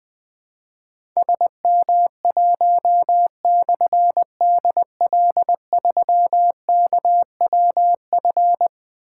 Morse code, heard as a keyed tone, about 20 wpm, 700 Hz. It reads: SM1/DL3KWF
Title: Difficult Call Signs 20wpm